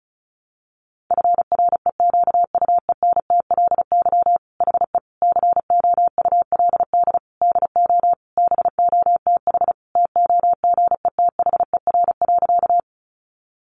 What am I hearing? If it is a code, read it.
Morse code, heard as a keyed tone, about 34 wpm, 700 Hz. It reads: FREQUENTLY HE COULD DO BOTH TOGETHER.
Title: Sentences from Top 1300 Words 35wpm